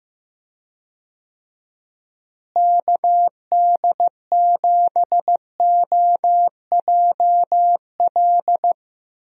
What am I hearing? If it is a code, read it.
Morse code, heard as a keyed tone, about 15 wpm, 700 Hz. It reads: KD7OJL
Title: USA Call Signs - 6 Characters - Repeated 2x - 15wpm